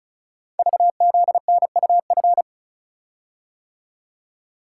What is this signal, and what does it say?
Morse code, heard as a keyed tone, about 36 wpm, 700 Hz. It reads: VZNUF
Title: Code Groups - 5 Letter:Numbers - 35wpm